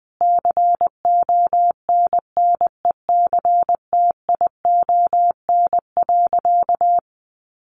Morse code, heard as a keyed tone, about 20 wpm, 700 Hz. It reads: CONNECTION.
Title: Sentences from Top 1300 Words 20wpm